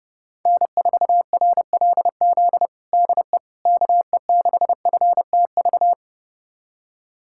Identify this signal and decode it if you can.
Morse code, heard as a keyed tone, about 30 wpm, 700 Hz. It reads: N4RLZ DE KE6FTV
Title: USA Callsign DE Callsign 30wpm